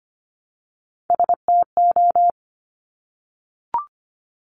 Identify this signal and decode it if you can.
Morse code, heard as a keyed tone, about 25 wpm, 700 Hz. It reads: STO
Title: Top Three Letter Combinations in Words 25wpm